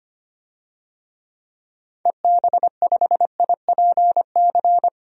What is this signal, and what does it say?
Morse code, heard as a keyed tone, about 25 wpm, 700 Hz. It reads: EB5IPC
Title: DXCC-100 Call Signs - 6 Characters - 25wpm